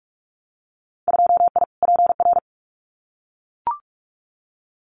Morse code, heard as a keyed tone, about 44 wpm, 700 Hz. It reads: 2I PR
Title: Winter Field Day Exchanges Encore 45wpm